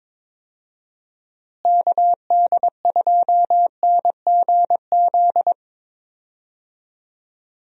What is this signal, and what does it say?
Morse code, heard as a keyed tone, about 22 wpm, 700 Hz. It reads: KD2NGZ
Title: USA Call Signs - 6 Characters - Repeated 2x - 22wpm